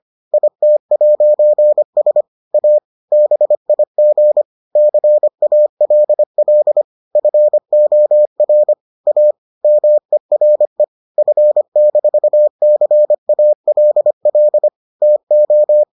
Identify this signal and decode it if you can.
Morse code, heard as a keyed tone, about 25 wpm, 600 Hz. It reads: IT'S A BIG CALL FOR A MERE F-CALL TO